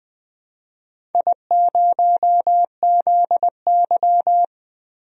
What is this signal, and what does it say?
Morse code, heard as a keyed tone, about 20 wpm, 700 Hz. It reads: I0ZY
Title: DXCC-100 Call Signs - 4 Characters - Repeated 2x - 20wpm